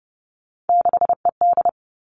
Morse code, heard as a keyed tone, about 30 wpm, 700 Hz. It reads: 6ED
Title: Three Challenging Letter-Number Combinations 30wpm